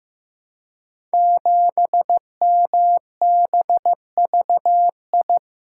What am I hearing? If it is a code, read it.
Morse code, heard as a keyed tone, about 15 wpm, 700 Hz. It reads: 7MBVI
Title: Code Groups - 5 Letter:Numbers - 15wpm